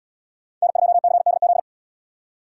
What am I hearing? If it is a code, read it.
Morse code, heard as a keyed tone, about 77 wpm, 700 Hz. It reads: N2CDZ